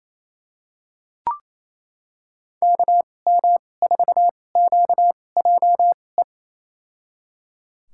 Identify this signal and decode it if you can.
Morse code, heard as a keyed tone, about 28 wpm, spaced as for 20 wpm, 700 Hz. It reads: KM4QJE
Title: USA Call Signs - 6 Characters - 2x Character Spacing - 28wpm